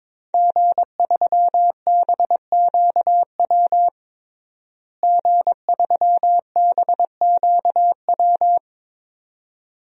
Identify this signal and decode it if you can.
Morse code, heard as a keyed tone, about 22 wpm, 700 Hz. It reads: G3BQW G3BQW
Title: DXCC-100 Call Signs - 5 Characters - Repeated 2x - 22wpm